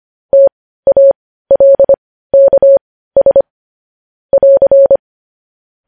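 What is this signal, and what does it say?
Morse code, heard as a keyed tone, about 25 wpm, spaced as for 15 wpm, 550 Hz. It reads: TALKS <AR>
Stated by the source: News headlines in Morse code at 15 words per minute, characters sent at 25 WPM.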